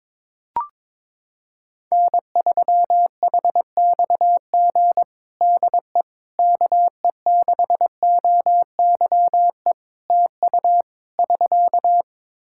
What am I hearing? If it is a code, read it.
Morse code, heard as a keyed tone, about 21 wpm, 700 Hz. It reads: N3HXG DE KE6OYE TU <SK>
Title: Practice QSO - Exchange 4 - Ending - 22wpm